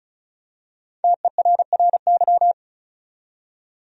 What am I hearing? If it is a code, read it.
Morse code, heard as a keyed tone, about 36 wpm, 700 Hz. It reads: TERRY
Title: USA Common Names - Max 5 Letters - 35wpm